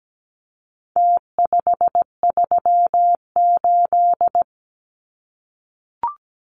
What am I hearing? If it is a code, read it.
Morse code, heard as a keyed tone, about 17 wpm, 700 Hz. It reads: T538
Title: Numbers - Contest Style - 4-Digit 17wpm